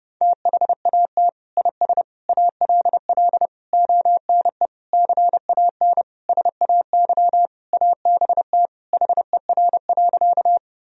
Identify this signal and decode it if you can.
Morse code, heard as a keyed tone, about 30 wpm, 700 Hz. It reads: THAT IS ALL ONE CAN SAY ABT HER.
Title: Sentences from Top 200 Words and Abbreviations 30wpm